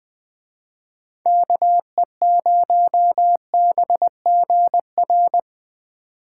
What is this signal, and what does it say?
Morse code, heard as a keyed tone, about 20 wpm, 700 Hz. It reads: KE0BGR